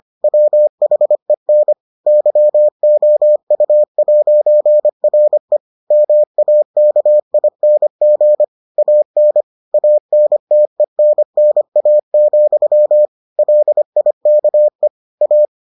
Morse code, heard as a keyed tone, about 25 wpm, 600 Hz. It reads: WHEN YOU'RE MAKING AN ANTENNA, LIKE A